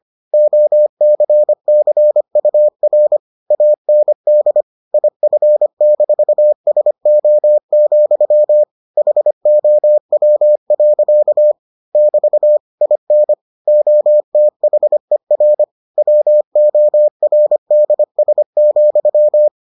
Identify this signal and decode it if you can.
Morse code, heard as a keyed tone, about 25 wpm, 600 Hz. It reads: OCCUR AND IF-SO, HOW. <BT> IN OTHER WORDS,